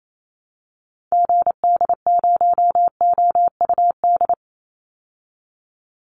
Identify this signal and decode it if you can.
Morse code, heard as a keyed tone, about 28 wpm, 700 Hz. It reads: GD0OUD
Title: Call Signs Repeated 2x - 28wpm